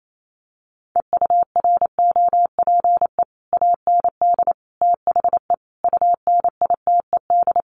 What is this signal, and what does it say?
Morse code, heard as a keyed tone, about 28 wpm, 700 Hz. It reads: EUROPE AND THE UNITED
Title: Sets of 5 Words Encore2 28wpm